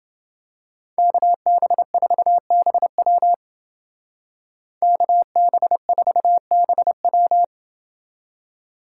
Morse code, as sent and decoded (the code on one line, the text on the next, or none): -.- -... ....- -... .-- / -.- -... ....- -... .--
KB4BW KB4BW